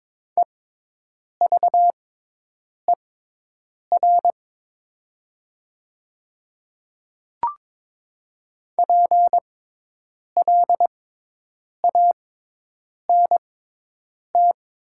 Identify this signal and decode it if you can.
Morse code, heard as a keyed tone, about 22 wpm, spaced as for 8 wpm, 700 Hz. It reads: EVER PLANT
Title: 6x Character Spacing - Top 500 Words - 22wpm